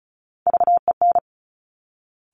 Morse code, heard as a keyed tone, about 36 wpm, 700 Hz. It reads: VEN
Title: Top Three Letter Combinations in Words 35wpm